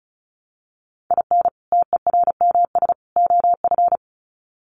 Morse code, heard as a keyed tone, about 34 wpm, 700 Hz. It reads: IN TERMS OF
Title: Sets of 3 Words 35wpm